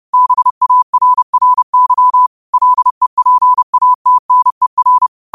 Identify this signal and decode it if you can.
Morse code, heard as a keyed tone, about 30 wpm, 1000 Hz. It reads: BARRY LEPATNER
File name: qotdfiles / 30wpm